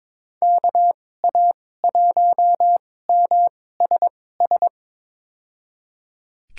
Morse code, heard as a keyed tone, about 21 wpm, spaced as for 16 wpm, 700 Hz. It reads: KA1MSS
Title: USA Call Signs - 6 Characters - 2x Character Spacing - 22wpm